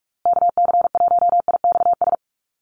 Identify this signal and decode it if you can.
Morse code, heard as a keyed tone, about 44 wpm, 700 Hz. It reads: KC1IXS
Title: USA Call Signs Repeated 3x 45wpm